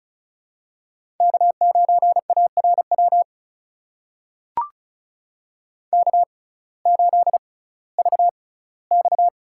Morse code, heard as a keyed tone, about 36 wpm, spaced as for 19 wpm, 700 Hz. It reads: K9ARW K8VX
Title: USA Call Signs - 5 Characters - 6x Character Spacing - 35wpm